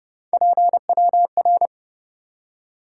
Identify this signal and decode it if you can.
Morse code, heard as a keyed tone, about 37 wpm, 700 Hz. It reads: PWR
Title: Speed-Racing Top 90 QSO Elements 30wpm